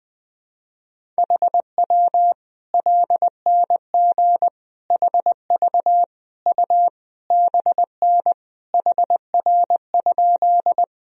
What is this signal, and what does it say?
Morse code, heard as a keyed tone, about 20 wpm, 700 Hz. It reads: HW LNG HV U BN HR?
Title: Sentences from Top 200 Words and Abbreviations 20wpm